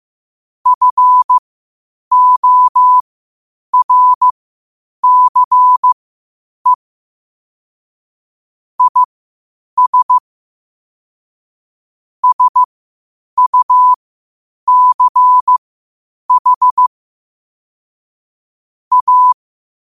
Morse code, heard as a keyed tone, about 15 wpm, spaced as for 8 wpm, 1000 Hz. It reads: FORCE IS SUCH A
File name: qotdfiles / 8wpm